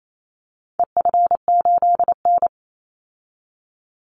Morse code, heard as a keyed tone, about 28 wpm, 700 Hz. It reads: EF8N
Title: Call Signs - 28wpm